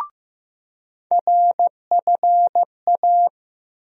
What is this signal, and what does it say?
Morse code, heard as a keyed tone, about 15 wpm, 700 Hz. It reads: RFA
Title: USA Call Sign Postfixes - 3 Letters 15wpm